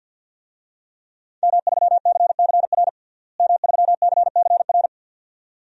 Morse code, heard as a keyed tone, about 52 wpm, 700 Hz. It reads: M3CCR M3CCR
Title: DXCC-100 Call Signs - 5 Characters - Repeated 2x - 50wpm